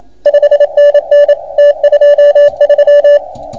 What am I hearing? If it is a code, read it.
Morse code, heard as a keyed tone, about 27 wpm, 580 Hz. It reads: 5NN T23